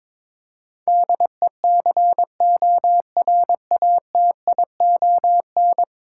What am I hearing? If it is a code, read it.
Morse code, heard as a keyed tone, about 22 wpm, 700 Hz. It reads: DECORATION
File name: practice / 22wpm / 10 Letter Words - First 3 Letters Given - 22wpm